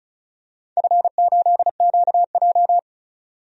Morse code, heard as a keyed tone, about 36 wpm, 690 Hz. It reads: F8QJ
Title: DXCC-100 Call Signs - 4 Characters - 35wpm